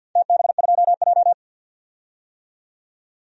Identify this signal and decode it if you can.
Morse code, heard as a keyed tone, about 52 wpm, 700 Hz. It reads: TB2J
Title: DXCC-100 Call Signs - 4 Characters - 50wpm